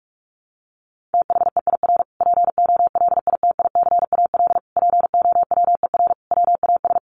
Morse code, heard as a keyed tone, about 44 wpm, 700 Hz. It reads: THEIR POLITICAL POWER WAS
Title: Sentences from Top 1300 Words 45wpm